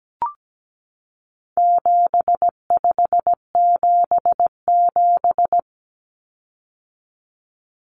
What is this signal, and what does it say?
Morse code, heard as a keyed tone, about 17 wpm, 700 Hz. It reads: 7577
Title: Common Cut Numbers - 4-Digit 17wpm